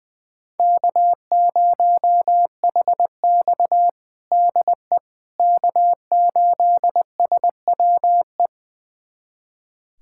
Morse code, heard as a keyed tone, about 20 wpm, 700 Hz. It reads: K0HX DE K8SWE
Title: USA Callsign DE Callsign 20wpm